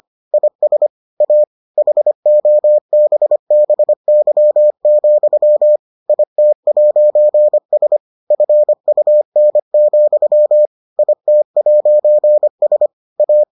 Morse code, heard as a keyed tone, about 25 wpm, 600 Hz. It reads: IS A HOBBY, IT'S FUN, IT'S A